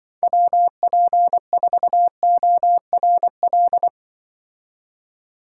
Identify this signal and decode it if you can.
Morse code, heard as a keyed tone, about 25 wpm, 700 Hz. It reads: WP4ORL
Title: Speed Racing - USA 2x3 Call Signs 20wpm